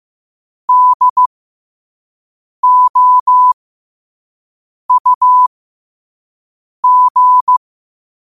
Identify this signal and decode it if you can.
Morse code, heard as a keyed tone, about 15 wpm, spaced as for 5 wpm, 1000 Hz. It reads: DOUG
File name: qotdfiles / 5wpm